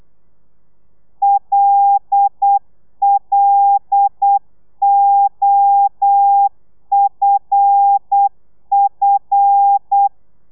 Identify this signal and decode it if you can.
Morse code, heard as a keyed tone, about 8 wpm, 800 Hz. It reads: LLOFF
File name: telegrafia / i2tfj / Lez5 35